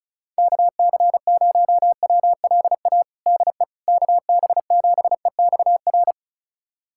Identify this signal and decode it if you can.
Morse code, heard as a keyed tone, about 36 wpm, 700 Hz. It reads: KC0WLA DE KB7EXR
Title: USA Callsign DE Callsign 35wpm